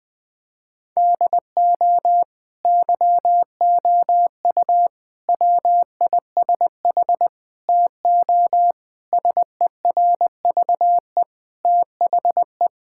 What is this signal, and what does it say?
Morse code, heard as a keyed tone, about 20 wpm, 700 Hz. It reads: DO YOU WISH TO SERVE THE
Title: Sentences from Top 1600 Words 20wpm